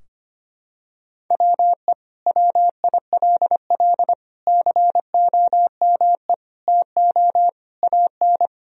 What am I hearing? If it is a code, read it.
Morse code, heard as a keyed tone, about 25 wpm, 700 Hz. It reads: WE WILL COME TO AN
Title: Sentences from Top 1000 Words - Review Part 3 - 25wpm